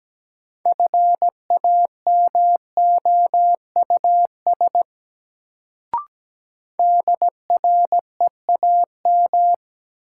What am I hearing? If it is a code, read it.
Morse code, heard as a keyed tone, about 17 wpm, 700 Hz. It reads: FAMOUS DREAM
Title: Sentences from Top 1500 Words - New Word Drill Alt - 17wpm